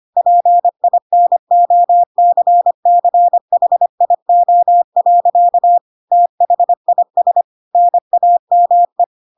Morse code, heard as a keyed tone, about 25 wpm, 700 Hz. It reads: PINOCCHIO. THIS NAME
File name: Pinocchio / Pinocchio ch3 25 25 ik1hge